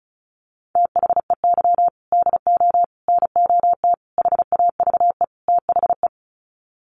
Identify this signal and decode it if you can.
Morse code, heard as a keyed tone, about 34 wpm, 700 Hz. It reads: THEY DO NOT HAVE THE
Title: Sets of 5 Words 35wpm